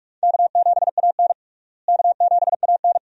Morse code, heard as a keyed tone, about 46 wpm, 700 Hz. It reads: K7AN K7AN
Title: USA Call Signs - 4 Characters - Repeated 2x - 45wpm